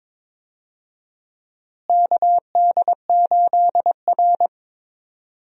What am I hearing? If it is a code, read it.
Morse code, heard as a keyed tone, about 23 wpm, 700 Hz. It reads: KD8R
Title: USA Call Signs - 4 Characters - Repeated 3x - 22wpm